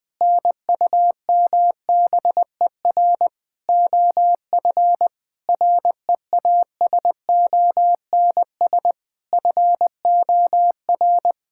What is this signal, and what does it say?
Morse code, heard as a keyed tone, about 20 wpm, 700 Hz. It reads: NUMBER OF REASONS FOR
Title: Sentences with 5-Word Phrases - 20wpm